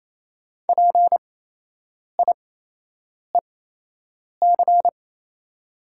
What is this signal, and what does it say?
Morse code, heard as a keyed tone, about 27 wpm, spaced as for 8 wpm, 700 Hz. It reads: PIEC
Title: 8x Character Spacing - Top 500 Words - 28wpm